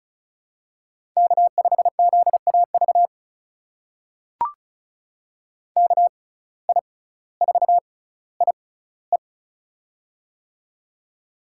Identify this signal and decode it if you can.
Morse code, heard as a keyed tone, about 36 wpm, spaced as for 18 wpm, 700 Hz. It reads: K5ZAV KI4IE
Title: USA Call Signs - 5 Characters - 6x Character Spacing - 35wpm